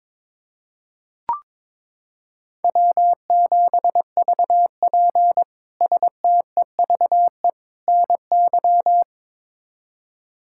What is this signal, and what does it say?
Morse code, heard as a keyed tone, about 22 wpm, 700 Hz. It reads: W7VP STEVE NY
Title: K1USN SST QSO Practice - 22wpm